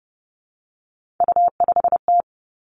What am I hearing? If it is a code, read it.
Morse code, heard as a keyed tone, about 30 wpm, 700 Hz. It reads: U5T
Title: Three Challenging Letter-Number Combinations 30wpm